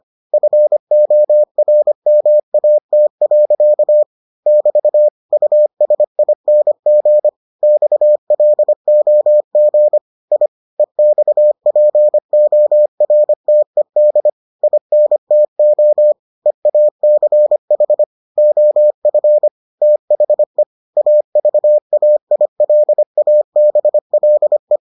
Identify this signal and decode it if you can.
Morse code, heard as a keyed tone, about 25 wpm, 600 Hz. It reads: FORMAT. <BT> USING XLOG I EXPORTED INTO EACH OF THE AVAILABLE